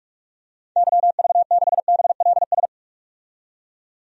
Morse code, heard as a keyed tone, about 44 wpm, 690 Hz. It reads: YV6BLS
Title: DXCC-100 Call Signs - 6 Characters - 45wpm